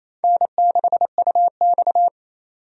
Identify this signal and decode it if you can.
Morse code, heard as a keyed tone, about 28 wpm, 700 Hz. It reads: N6UX
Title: USA Call Signs - 4 Characters - 28wpm